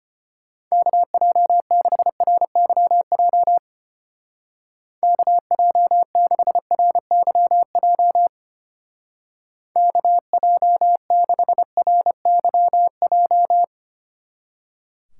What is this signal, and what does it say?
Morse code, heard as a keyed tone, about 31 wpm, 700 Hz. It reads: KJ6RYJ KJ6RYJ KJ6RYJ